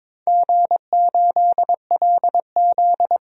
Morse code, heard as a keyed tone, about 22 wpm, 700 Hz. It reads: G8LZ
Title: DXCC-100 Call Signs - 4 Characters - 22wpm